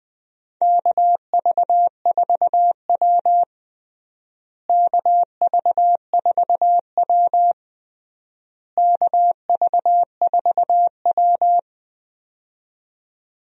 Morse code, heard as a keyed tone, about 20 wpm, 700 Hz. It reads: KV4W KV4W KV4W
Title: USA Call Signs - 4 Characters - Repeated 3x - 20wpm